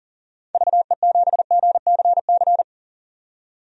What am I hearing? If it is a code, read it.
Morse code, heard as a keyed tone, about 38 wpm, 700 Hz. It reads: VE7GCC
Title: DXCC-100 Call Signs - 6 Characters - 40wpm